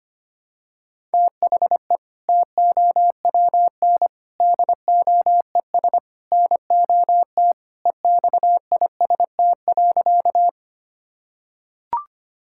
Morse code, heard as a keyed tone, about 25 wpm, 700 Hz. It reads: THE TOWN DOES NOT EXIST.
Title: Sentences from Top 1000 Words - Review Part 5 - 25wpm